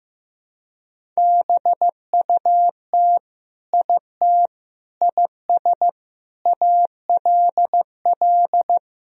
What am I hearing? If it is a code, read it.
Morse code, heard as a keyed tone, about 15 wpm, 700 Hz. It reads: BUT IT IS ALL
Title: Sentences from Top 1000 Words - Review Part 1 - 15wpm